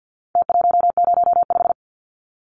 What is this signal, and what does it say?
Morse code, heard as a keyed tone, about 50 wpm, 700 Hz. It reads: T105
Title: Numbers - Contest Style - 4-Digit 50wpm